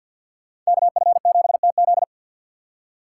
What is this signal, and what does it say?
Morse code, heard as a keyed tone, about 61 wpm, 700 Hz. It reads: KF7TZ
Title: Speed Racing - USA 2x2 Call Signs 50wpm